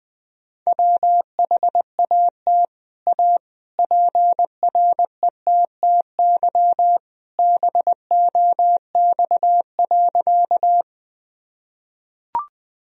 Morse code, heard as a keyed tone, about 20 wpm, 700 Hz. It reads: WHAT A PRETTY BOX.
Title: Sentences from Top 1000 Words - Review Part 2 - 20wpm